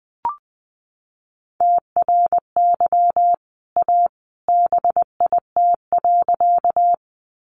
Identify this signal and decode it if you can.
Morse code, heard as a keyed tone, about 20 wpm, 700 Hz. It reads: TRY A BIT.
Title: Sentences from Top 900 Words 20wpm